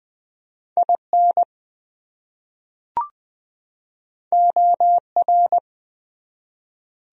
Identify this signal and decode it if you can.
Morse code, heard as a keyed tone, about 20 wpm, 700 Hz. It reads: IN OR
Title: Top 100 Words 20wpm